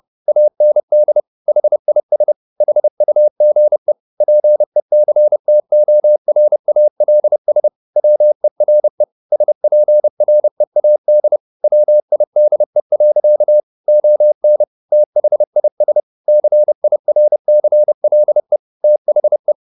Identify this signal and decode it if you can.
Morse code, heard as a keyed tone, about 30 wpm, 600 Hz. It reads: AND HIS HUGE PECTORALS WERE SPREAD WIDE. ON THIS CIRCLE THE